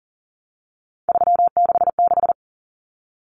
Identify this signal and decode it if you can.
Morse code, heard as a keyed tone, about 40 wpm, 700 Hz. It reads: 366
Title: Numbers - 3-Digit 40wpm